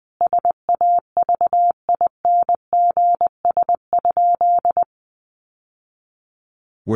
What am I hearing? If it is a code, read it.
Morse code, heard as a keyed tone, about 20 wpm, 700 Hz. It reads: SAVINGS?
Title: Sentences from Top 1200 Words 20wpm